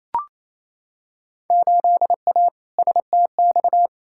Morse code, heard as a keyed tone, about 28 wpm, 700 Hz. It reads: 8A STX
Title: Field Day Exchanges Encore - 28wpm